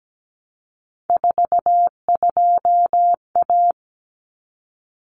Morse code, heard as a keyed tone, about 17 wpm, 700 Hz. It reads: 42A
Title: Common Cut Numbers - 3-Digit 17wpm